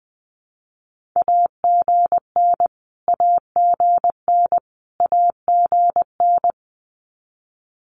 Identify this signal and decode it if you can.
Morse code, heard as a keyed tone, about 20 wpm, 700 Hz. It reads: AGN AGN AGN
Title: Abbreviations Repeated 3x - 20wpm